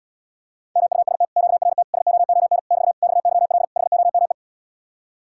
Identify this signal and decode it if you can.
Morse code, heard as a keyed tone, about 79 wpm, 700 Hz. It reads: MUST QRT SOON <BT> XYL HOME